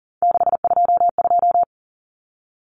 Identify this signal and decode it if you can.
Morse code, heard as a keyed tone, about 40 wpm, 700 Hz. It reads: 622
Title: Common Cut Numbers - 3-Digit 40wpm